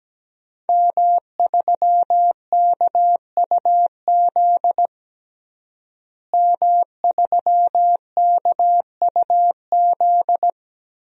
Morse code, heard as a keyed tone, about 17 wpm, 700 Hz. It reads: M3KUZ M3KUZ
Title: DXCC-100 Call Signs - 5 Characters - Repeated 2x - 17wpm